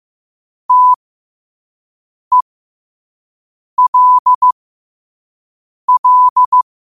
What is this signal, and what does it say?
Morse code, heard as a keyed tone, about 15 wpm, spaced as for 5 wpm, 1000 Hz. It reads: TELL